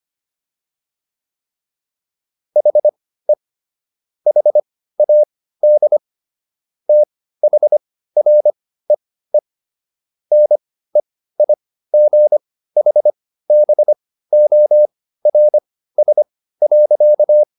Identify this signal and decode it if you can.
Morse code, heard as a keyed tone, about 25 wpm, spaced as for 15 wpm, 600 Hz. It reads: HE HAD THREE NEIGHBORS.